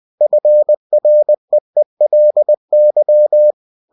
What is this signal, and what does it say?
Morse code, heard as a keyed tone, about 20 wpm, 600 Hz. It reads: FREELY
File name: cw-books / winnie-the-pooh / winnie-20wpm-page-0122